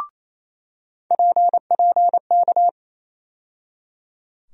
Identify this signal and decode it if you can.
Morse code, heard as a keyed tone, about 28 wpm, 700 Hz. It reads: PPK
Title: USA Call Sign Postfixes - 3 Letters 28wpm